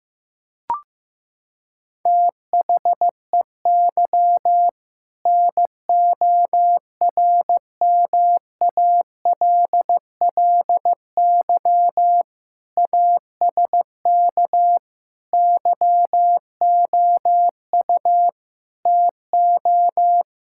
Morse code, heard as a keyed tone, about 16 wpm, 700 Hz. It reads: THEY NORMALLY ASK YOU TO
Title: Sentences from Top 1000 Words - Review Part 6 - 15wpm